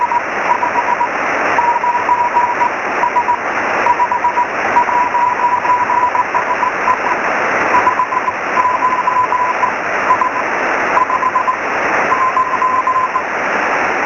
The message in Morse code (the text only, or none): none